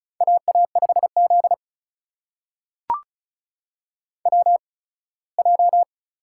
Morse code, heard as a keyed tone, about 36 wpm, spaced as for 21 wpm, 700 Hz. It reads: AA5Z WJ
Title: USA Call Signs - 4 Characters - 8x Character Spacing - 35wpm